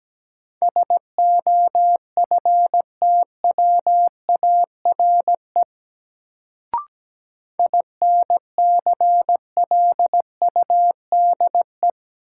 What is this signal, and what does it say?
Morse code, heard as a keyed tone, about 17 wpm, 700 Hz. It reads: SOFTWARE INCLUDE
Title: Sentences from Top 1000 Words - Review Part 3 - New Word Drill - 17wpm